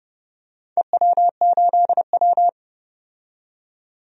Morse code, heard as a keyed tone, about 29 wpm, 700 Hz. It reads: EW8W
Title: DXCC-100 Call Signs - 4 Characters - 30wpm